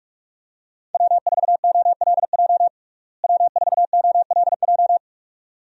Morse code, heard as a keyed tone, about 46 wpm, 700 Hz. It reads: W4OLJ W4OLJ
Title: USA Call Signs - 5 Characters - Repeated 2x - 45wpm